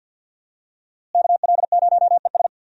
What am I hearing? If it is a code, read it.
Morse code, heard as a keyed tone, about 52 wpm, 700 Hz. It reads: KL0ES
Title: USA Call Signs - 5 Characters - Repeated 2x - 50wpm